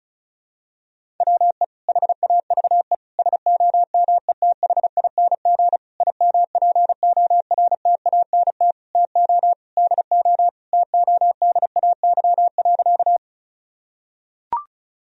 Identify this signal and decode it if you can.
Morse code, heard as a keyed tone, about 36 wpm, 700 Hz. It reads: WE HAVE SOMETHING IMPORTANT TO DO TODAY.
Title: Sentences from Top 1000 Words - Review Part 4 - 35wpm